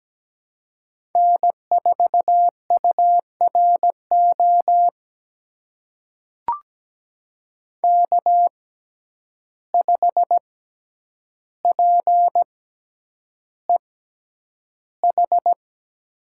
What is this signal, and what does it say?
Morse code, heard as a keyed tone, about 17 wpm, spaced as for 9 wpm, 700 Hz. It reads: N4URO K5PEH
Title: USA Call Signs - 5 Characters - 6x Character Spacing - 17wpm